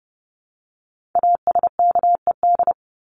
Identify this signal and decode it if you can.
Morse code, heard as a keyed tone, about 29 wpm, 700 Hz. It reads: ASKED
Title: ED Suffix - Top 100 Words with Base Words 30wpm